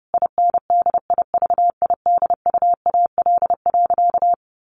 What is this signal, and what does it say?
Morse code, heard as a keyed tone, about 30 wpm, 700 Hz. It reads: INDIVIDUAL.
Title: Sentences from Top 1500 Words 30wpm